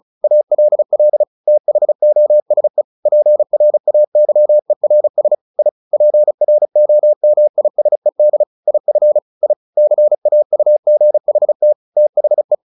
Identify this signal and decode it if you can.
Morse code, heard as a keyed tone, about 34 wpm, 600 Hz. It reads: ALL THOSE PRAYERS I PROMISED IF I CAUGHT THE
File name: the old man and the sea 2 35wpm